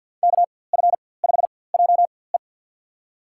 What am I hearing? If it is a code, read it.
Morse code, heard as a keyed tone, about 46 wpm, spaced as for 23 wpm, 700 Hz. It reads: KF5JE
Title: USA Call Signs - 5 Characters - 4x Character Spacing - 50wpm